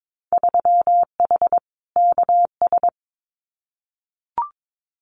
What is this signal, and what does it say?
Morse code, heard as a keyed tone, about 22 wpm, 700 Hz. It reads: 3H KS
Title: Winter Field Day Exchanges 22wpm